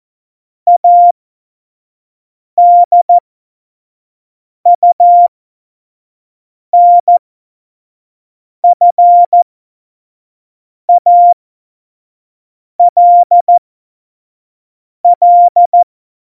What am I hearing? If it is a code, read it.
Morse code, heard as a keyed tone, about 14 wpm, spaced as for 5 wpm, 700 Hz. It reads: ADUNFALL
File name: Morsekurs / Witze / Witz 01. September 2020, 25 F70